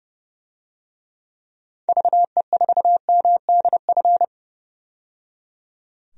Morse code, heard as a keyed tone, about 30 wpm, 700 Hz. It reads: VE4MDF
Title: DXCC-100 Call Signs - 6 Characters - 30wpm